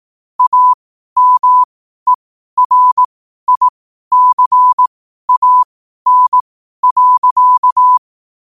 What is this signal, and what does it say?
Morse code, heard as a keyed tone, about 18 wpm, spaced as for 12 wpm, 1000 Hz. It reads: AMERICAN.
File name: qotdfiles / 12wpm